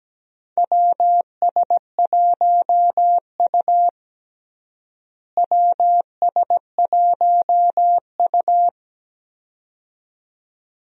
Morse code, heard as a keyed tone, about 17 wpm, 700 Hz. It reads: WS1U WS1U
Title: USA Call Signs - 4 Characters - Repeated 2x - 17wpm